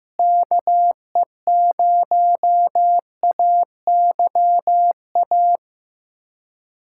Morse code, heard as a keyed tone, about 16 wpm, 700 Hz. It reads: KE0AYA
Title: USA Callsign DE Callsign - 6 Characters - 15wpm